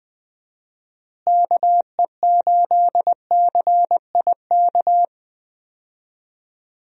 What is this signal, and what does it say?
Morse code, heard as a keyed tone, about 20 wpm, 700 Hz. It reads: KE8CIK
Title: USA Call Signs - 6 Characters - Repeated 3x - 20wpm